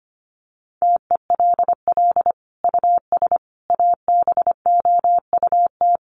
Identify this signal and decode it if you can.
Morse code, heard as a keyed tone, about 25 wpm, 700 Hz. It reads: TELL US ABOUT
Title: Sentences from Top 1300 Words 25wpm